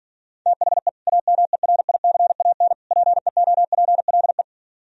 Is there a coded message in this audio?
Yes: Morse code from 0.5 to 4.4 s, heard as a keyed tone, about 51 wpm, 700 Hz. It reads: THE AMERICAN PEOPLE